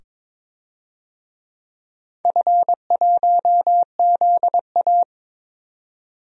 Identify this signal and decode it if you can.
Morse code, heard as a keyed tone, about 22 wpm, 700 Hz. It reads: F1ZA
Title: DXCC-100 Call Signs - 4 Characters - 22wpm